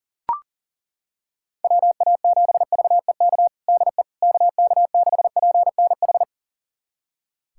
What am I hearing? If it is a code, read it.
Morse code, heard as a keyed tone, about 40 wpm, 700 Hz. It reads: WA7VEK DE KK6PNH
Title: USA Callsign DE Callsign - 6 Characters - 40wpm